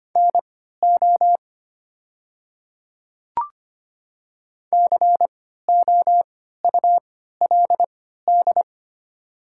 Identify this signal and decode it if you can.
Morse code, heard as a keyed tone, about 25 wpm, spaced as for 14 wpm, 700 Hz. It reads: NO COULD
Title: 3x Character Spacing - Top 100 Words - 25wpm